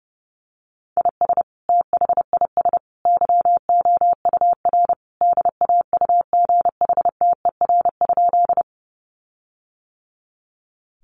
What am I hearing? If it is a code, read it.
Morse code, heard as a keyed tone, about 30 wpm, 700 Hz. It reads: IS THIS YOUR DAUGHTER?
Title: Sentences from Top 1500 Words 30wpm